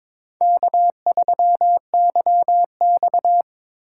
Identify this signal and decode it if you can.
Morse code, heard as a keyed tone, about 25 wpm, 700 Hz. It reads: K3YX
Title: Speed Racing - USA 1x2 Call Signs 22wpm